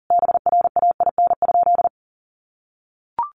Morse code, heard as a keyed tone, about 40 wpm, 700 Hz. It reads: BRAIN?
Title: Sentences from Top 1100 Words 40wpm